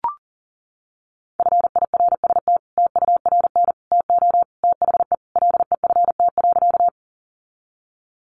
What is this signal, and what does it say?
Morse code, heard as a keyed tone, about 38 wpm, 700 Hz. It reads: FIRST TURN TO THE LEFT.
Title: Sentences from Top 300 Words 40wpm